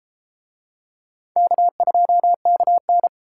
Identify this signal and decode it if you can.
Morse code, heard as a keyed tone, about 30 wpm, 700 Hz. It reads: K2KN